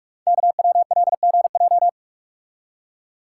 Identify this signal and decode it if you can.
Morse code, heard as a keyed tone, about 44 wpm, 690 Hz. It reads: KWLGJ
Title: Code Groups - 5 Letters - 45wpm